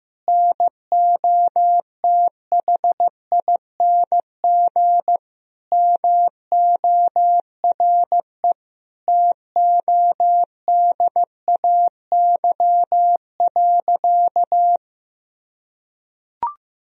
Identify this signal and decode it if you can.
Morse code, heard as a keyed tone, about 16 wpm, 700 Hz. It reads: NOTHING MORE TODAY.
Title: Sentences from Top 1000 Words - Review Part 4 - 15wpm